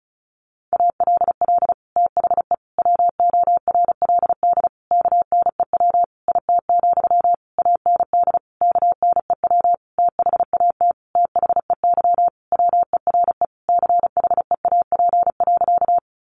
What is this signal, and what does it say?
Morse code, heard as a keyed tone, about 34 wpm, 700 Hz. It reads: ALL THE WORLD KNEW IT, AND KNEW THAT THEY WERE CHEAP.
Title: Sentences from Top 1200 Words 35wpm